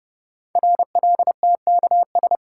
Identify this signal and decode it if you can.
Morse code, heard as a keyed tone, about 30 wpm, 700 Hz. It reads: RLTKS
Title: Code Groups - 5 Letters - 30wpm